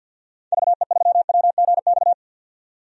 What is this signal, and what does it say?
Morse code, heard as a keyed tone, about 52 wpm, 700 Hz. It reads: VE3WGX
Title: DXCC-100 Call Signs - 6 Characters - 50wpm